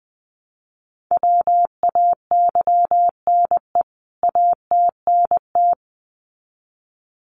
Morse code, heard as a keyed tone, about 20 wpm, 700 Hz. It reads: WAYNE ATNT
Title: CWT Exchange 20wpm